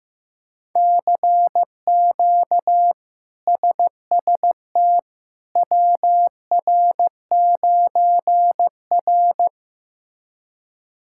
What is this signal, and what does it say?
Morse code, heard as a keyed tone, about 16 wpm, 700 Hz. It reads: CQ SST WR9R
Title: K1USN SST QSO Practice - 15wpm